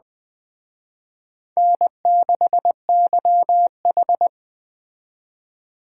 Morse code, heard as a keyed tone, about 21 wpm, 700 Hz. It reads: N6YH